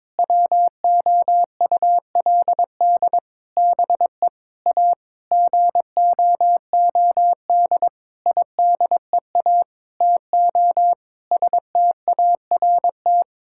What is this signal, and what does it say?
Morse code, heard as a keyed tone, about 22 wpm, 700 Hz. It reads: WOULD BE A GOOD IDEA TO START
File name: practice / 22wpm / Sentences with 8-Word Phrases - 22wpm